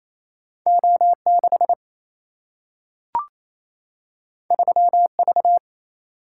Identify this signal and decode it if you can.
Morse code, heard as a keyed tone, about 28 wpm, 700 Hz. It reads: O 6 3 V
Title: Single-Letter Single-Number 28wpm